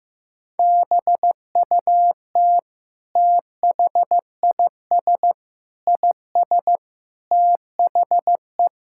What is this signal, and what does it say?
Morse code, heard as a keyed tone, about 15 wpm, 700 Hz. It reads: BUT THIS IS THE
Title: Sentences from Top 1000 Words - Review Part 5 - 15wpm